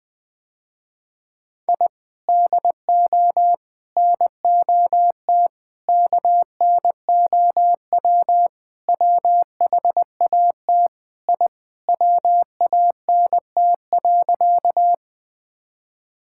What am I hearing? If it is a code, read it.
Morse code, heard as a keyed tone, about 20 wpm, 700 Hz. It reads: I DO NOT KNOW WHAT I WANT.
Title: Sentences with 5-Word Phrases - Spoken Beginning - 20wpm